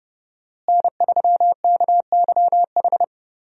Morse code, heard as a keyed tone, about 32 wpm, 700 Hz. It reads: N3KYH
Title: USA Call Signs - 5 Characters - Repeated 2x - 30wpm